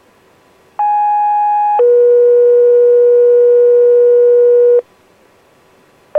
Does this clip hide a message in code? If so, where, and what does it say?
Yes: two-tone page from 0.8 to 4.8 s, heard as a one-second tone, then a three-second tone, about 830 then 480 Hz.